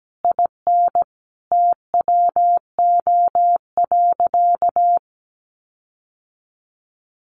Morse code, heard as a keyed tone, about 17 wpm, 700 Hz. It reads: IN TWO.
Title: Sentences from Top 1300 Words 17wpm